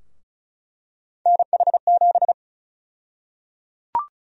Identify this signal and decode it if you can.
Morse code, heard as a keyed tone, about 34 wpm, 700 Hz. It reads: NH7
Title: USA Call Sign Prefixes - Repeated 2x - 35wpm